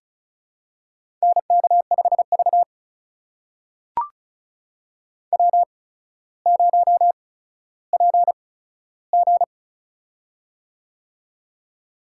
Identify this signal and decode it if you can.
Morse code, heard as a keyed tone, about 36 wpm, spaced as for 15 wpm, 700 Hz. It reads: NK5V W0PG
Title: USA Call Signs - 4 Characters - 8x Character Spacing - 35wpm